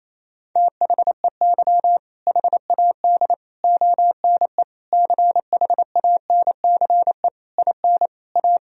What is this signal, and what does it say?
Morse code, heard as a keyed tone, about 28 wpm, 700 Hz. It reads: THEY HAD ONE CHANCE IN A
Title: Sentences from Top 1000 Words - Review Part 3 - 28wpm